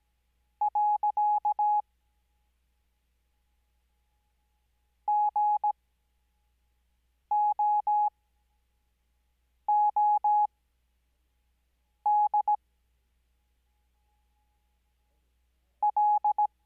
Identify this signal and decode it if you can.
Morse code, heard as a keyed tone, about 17 wpm, spaced as for 5 wpm, 840 Hz. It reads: . GOOD L
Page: Farnsworth Spacing: 15WPM Character speed Spaced at 5WPM.